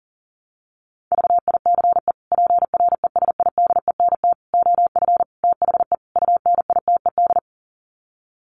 Morse code, heard as a keyed tone, about 40 wpm, 700 Hz. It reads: VICE PRESIDENT OF THE UNITED
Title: Sets of 5 Words Encore2 40wpm